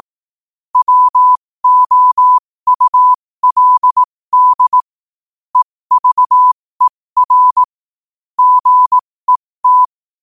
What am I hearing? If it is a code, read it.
Morse code, heard as a keyed tone, about 18 wpm, spaced as for 15 wpm, 1000 Hz. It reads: WOULD EVER GET
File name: qotdfiles / 15wpm